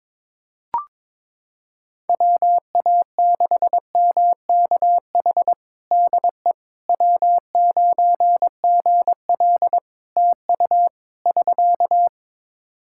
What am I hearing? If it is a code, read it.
Morse code, heard as a keyed tone, about 22 wpm, 700 Hz. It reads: WA6MKH DE W9GL TU <SK>
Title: Practice QSO - Exchange 4 - Ending - 22wpm